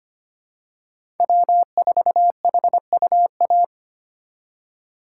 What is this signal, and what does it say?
Morse code, heard as a keyed tone, about 25 wpm, 700 Hz. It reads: W4HUA
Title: USA Call Signs - 5 Characters - 6x Character Spacing - 25wpm